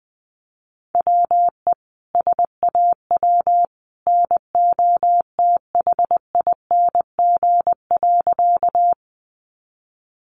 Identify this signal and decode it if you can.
Morse code, heard as a keyed tone, about 20 wpm, 700 Hz. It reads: WE SAW NOTHING.
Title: A Sound Of Thunder 20wpm